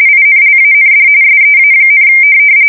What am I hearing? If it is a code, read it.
RTTY, about 45.45 baud. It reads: INLAND SEA,
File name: fsk170 / USS-Missouri